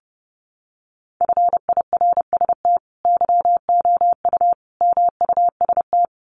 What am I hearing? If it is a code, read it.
Morse code, heard as a keyed tone, about 30 wpm, 700 Hz. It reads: FIRST YOU MUST
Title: Sentences from Top 1500 Words 30wpm